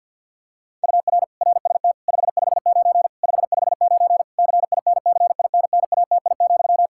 Morse code, heard as a keyed tone, about 51 wpm, 700 Hz. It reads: UR RST 559 559 CINCINNATI,
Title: Practice QSO - Exchange 1 - Beginning - 50wpm